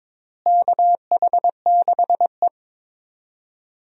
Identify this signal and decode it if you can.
Morse code, heard as a keyed tone, about 21 wpm, 700 Hz. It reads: KH6E
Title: DXCC-100 Call Signs - 4 Characters - Repeated 2x - 22wpm